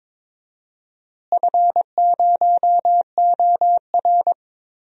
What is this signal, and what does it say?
Morse code, heard as a keyed tone, about 21 wpm, 700 Hz. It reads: F0OR
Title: DXCC-100 Call Signs - 4 Characters - Repeated 2x - 22wpm